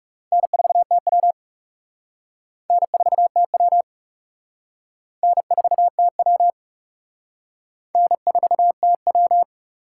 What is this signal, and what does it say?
Morse code, heard as a keyed tone, about 35 wpm, 700 Hz. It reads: N4TW N4TW N4TW N4TW
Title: Speed Racing - USA 1x2 Call Signs 30wpm